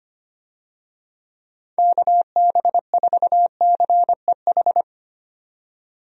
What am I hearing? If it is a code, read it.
Morse code, heard as a keyed tone, about 25 wpm, 700 Hz. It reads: KB4CEH
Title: USA Call Signs - 6 Characters - 25wpm